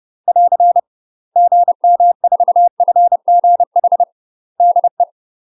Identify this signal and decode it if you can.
Morse code, heard as a keyed tone, about 30 wpm, 700 Hz. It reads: <AR> GM4FGH DE
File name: UK MASTER QSOs TRAINING / 30WPM / UK MASTER QSOs 0003